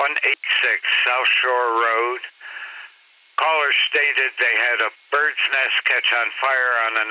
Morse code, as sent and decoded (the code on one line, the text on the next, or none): none